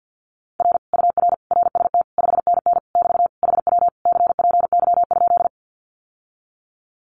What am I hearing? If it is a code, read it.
Morse code, heard as a keyed tone, about 48 wpm, 700 Hz. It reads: R UR RST 5NN <BT> HW CPY?
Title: Common QSO Phrases Encore 50wpm